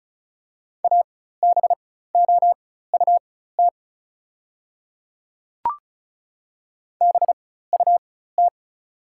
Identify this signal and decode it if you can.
Morse code, heard as a keyed tone, about 36 wpm, spaced as for 16 wpm, 700 Hz. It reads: ABOUT BUT
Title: 4x Character Spacing - Top 200 Words - 35wpm